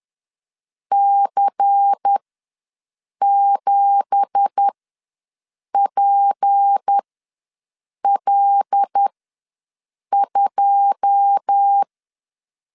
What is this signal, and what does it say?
Morse code, heard as a keyed tone, about 10 wpm, spaced as for 6 wpm, 800 Hz. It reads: C7PL2
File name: LEZIONI / 9 2867 ws 10wpm 4215